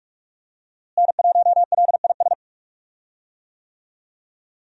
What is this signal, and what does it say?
Morse code, heard as a keyed tone, about 46 wpm, 700 Hz. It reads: N1LIS
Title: USA Call Signs - 5 Characters - Repeated 3x - 45wpm